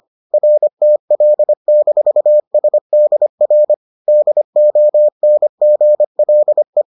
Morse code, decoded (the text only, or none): RTL-SDR DONGLE